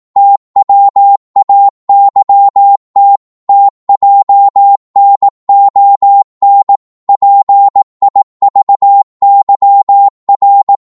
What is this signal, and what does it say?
Morse code, heard as a keyed tone, about 18 wpm, 800 Hz. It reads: TWAYT TJNON PIVYR
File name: Morse / ltr-18wpm